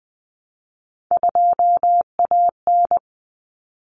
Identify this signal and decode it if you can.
Morse code, heard as a keyed tone, about 20 wpm, 700 Hz. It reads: 2AN
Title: Common Cut Numbers - 3-Digit 20wpm